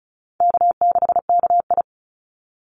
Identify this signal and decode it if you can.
Morse code, heard as a keyed tone, about 36 wpm, 700 Hz. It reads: K6KI
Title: Call Signs - 35wpm